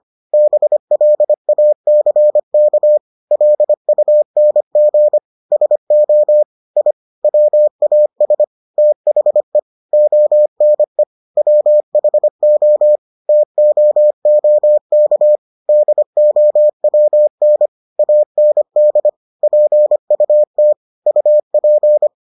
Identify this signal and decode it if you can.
Morse code, heard as a keyed tone, about 25 wpm, 600 Hz. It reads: BLACK LUNG SO I WAS THE ONE WHO TOOK DOWN AND PUT UP